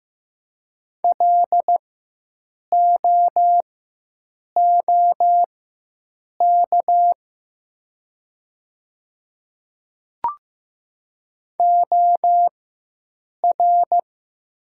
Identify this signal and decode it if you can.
Morse code, heard as a keyed tone, about 15 wpm, spaced as for 7 wpm, 700 Hz. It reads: LOOK OR
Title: 4x Character Spacing - Top 100 Words - 15wpm